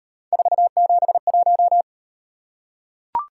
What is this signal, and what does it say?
Morse code, heard as a keyed tone, about 36 wpm, 700 Hz. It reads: 471